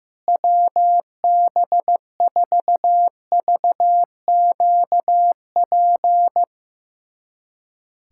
Morse code, heard as a keyed tone, about 15 wpm, 700 Hz. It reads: WB4VQP
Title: USA Callsign DE Callsign - 6 Characters - 15wpm